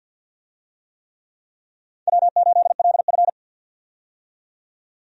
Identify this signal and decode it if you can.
Morse code, heard as a keyed tone, about 52 wpm, 700 Hz. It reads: W8LF
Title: USA Call Signs - 4 Characters - 50wpm